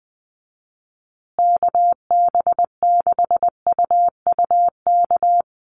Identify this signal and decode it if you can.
Morse code, heard as a keyed tone, about 20 wpm, 700 Hz. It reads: KB6UUK
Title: USA Call Signs Repeated 3x 20wpm